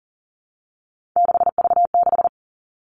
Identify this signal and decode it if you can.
Morse code, heard as a keyed tone, about 41 wpm, 700 Hz. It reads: 6V6
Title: Three Challenging Letter-Number Combinations 40wpm